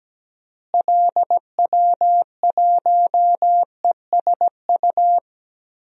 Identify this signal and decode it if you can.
Morse code, heard as a keyed tone, about 17 wpm, 700 Hz. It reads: LW1ESU
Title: DXCC-100 Call Signs - 6 Characters - Repeated 3x - 17wpm